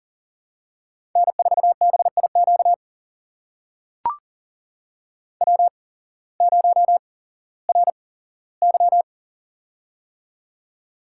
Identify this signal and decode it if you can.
Morse code, heard as a keyed tone, about 37 wpm, spaced as for 18 wpm, 700 Hz. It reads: N4BIQ W0RY
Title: USA Call Signs - 8x Character Spacing - 40wpm